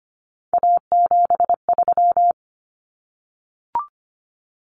Morse code, heard as a keyed tone, about 25 wpm, 700 Hz. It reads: A73
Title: Common Cut Numbers - 3-Digit 25wpm